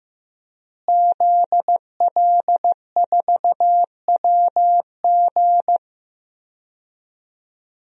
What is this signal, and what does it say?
Morse code, heard as a keyed tone, about 15 wpm, 700 Hz. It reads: ZL4WG
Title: DXCC-100 Call Signs - 5 Characters - Repeated 3x - 15wpm